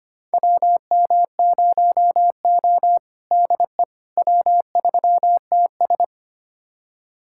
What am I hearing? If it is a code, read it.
Morse code, heard as a keyed tone, about 25 wpm, 700 Hz. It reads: WM0O DE W3TS
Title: USA Callsign DE Callsign - 4 Characters - 25wpm